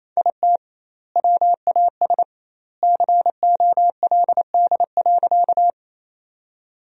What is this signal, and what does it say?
Morse code, heard as a keyed tone, about 28 wpm, 700 Hz. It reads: IT WAS COLD.
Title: 2x Word-Spacing - Sentences from Top 400 Words - 28wpm